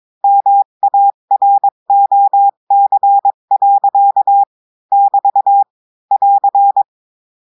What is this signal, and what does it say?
Morse code, heard as a keyed tone, about 22 wpm, 800 Hz. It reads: MAROC. <BT> <AR>
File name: cours-du-lundi files / cours-1580-texte-clair-20-mpm